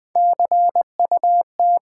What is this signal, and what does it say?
Morse code, heard as a keyed tone, about 20 wpm, 700 Hz. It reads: CUT
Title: Sentences from Top 300 Words - New Word Drill - 20wpm